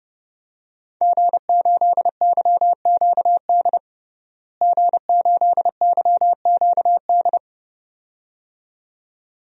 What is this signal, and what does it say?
Morse code, heard as a keyed tone, about 32 wpm, 700 Hz. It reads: G8YQD G8YQD
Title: DXCC-100 Call Signs - 5 Characters - Repeated 2x - 30wpm